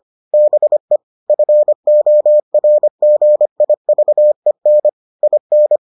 Morse code, heard as a keyed tone, about 25 wpm, 600 Hz. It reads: BE FORGIVEN IN